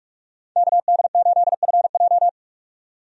Morse code, heard as a keyed tone, about 43 wpm, 700 Hz. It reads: KD8FJ
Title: USA Call Signs - 5 Characters - 45wpm